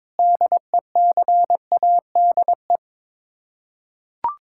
Morse code, heard as a keyed tone, about 22 wpm, 700 Hz. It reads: DECADE
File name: practice / 22wpm / Sets of 7 Words - 22wpm